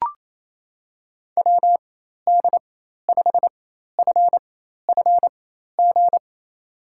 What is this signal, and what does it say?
Morse code, heard as a keyed tone, about 28 wpm, spaced as for 13 wpm, 700 Hz. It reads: WD5FFG
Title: USA Call Signs - 6 Characters - 4x Character Spacing - 28wpm